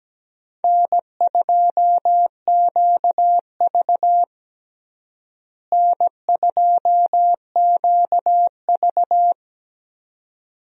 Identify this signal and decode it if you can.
Morse code, heard as a keyed tone, about 17 wpm, 700 Hz. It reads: N2QV N2QV
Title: USA Call Signs - 4 Characters - Repeated 2x - 17wpm